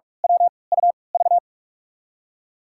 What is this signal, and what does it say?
Morse code, heard as a keyed tone, about 46 wpm, spaced as for 26 wpm, 700 Hz. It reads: WUV
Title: USA Call Signs - 5 Characters - 3x Character Spacing - 45wpm